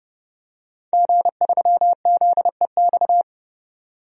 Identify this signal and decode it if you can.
Morse code, heard as a keyed tone, about 30 wpm, 700 Hz. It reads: G3ZEX
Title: DXCC-100 Call Signs - 5 Characters - 30wpm